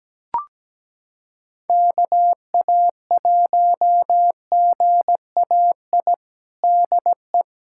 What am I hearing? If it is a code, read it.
Morse code, heard as a keyed tone, about 17 wpm, 700 Hz. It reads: KA1GAI DE
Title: USA Callsign DE Callsign - 6 Characters - 17wpm